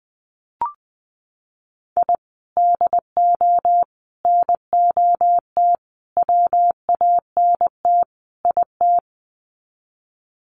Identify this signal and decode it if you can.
Morse code, heard as a keyed tone, about 20 wpm, 700 Hz. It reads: I DO NOT WANT IT
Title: Sets of 5 Words Encore2 20wpm